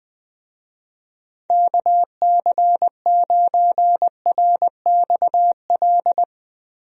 Morse code, heard as a keyed tone, about 20 wpm, 700 Hz. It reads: KC9RXL
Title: USA Call Signs - 6 Characters - 20wpm